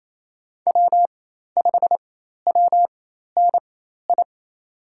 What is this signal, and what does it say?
Morse code, heard as a keyed tone, about 28 wpm, spaced as for 13 wpm, 700 Hz. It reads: W5WNI
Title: USA Call Signs - 5 Characters - 4x Character Spacing - 28wpm